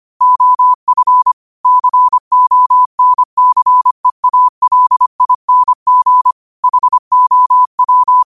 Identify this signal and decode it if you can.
Morse code, heard as a keyed tone, about 25 wpm, 1000 Hz. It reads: OF CONCEALING HOW
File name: qotdfiles / 25wpm